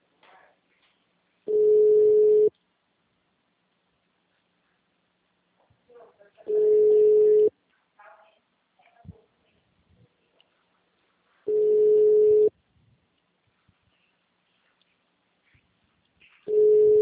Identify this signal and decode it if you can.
Ringback tone (Europe, 425 Hz), with speech.